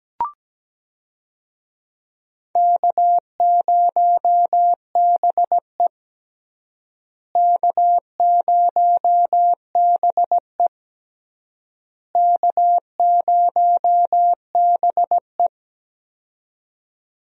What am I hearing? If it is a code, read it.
Morse code, heard as a keyed tone, about 17 wpm, 700 Hz. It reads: K0BE K0BE K0BE
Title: USA Call Signs - 4 Characters - Repeated 3x - 17wpm